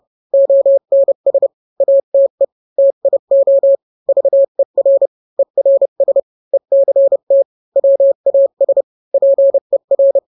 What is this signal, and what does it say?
Morse code, heard as a keyed tone, about 31 wpm, 550 Hz. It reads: ONS ATE TIO VER ERS ECT WAS PER